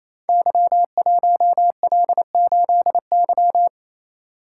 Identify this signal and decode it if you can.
Morse code, heard as a keyed tone, about 29 wpm, 700 Hz. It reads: Y1L8Y
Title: Code Groups - 5 Letter:Numbers - 28wpm